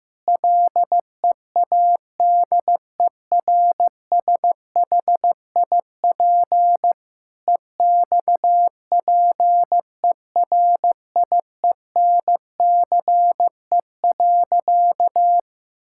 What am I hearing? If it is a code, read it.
Morse code, heard as a keyed tone, about 15 wpm, 700 Hz. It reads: LEADERSHIP EXPERIENCE.
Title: Sentences from Top 1600 Words 15wpm